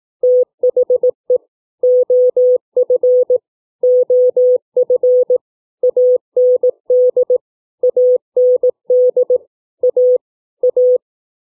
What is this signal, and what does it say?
Morse code, heard as a keyed tone, about 18 wpm, 500 Hz. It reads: THE OF OF AND AND A A
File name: CW / double words